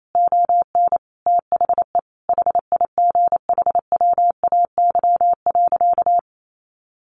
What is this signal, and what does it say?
Morse code, heard as a keyed tone, about 28 wpm, 700 Hz. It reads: ON THE HIGHWAY.
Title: Sentences from Top 1400 Words 28wpm